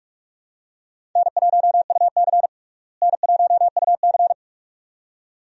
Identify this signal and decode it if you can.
Morse code, heard as a keyed tone, about 41 wpm, 700 Hz. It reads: N1UC N1UC
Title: USA Call Signs - 4 Characters - Repeated 2x - 45wpm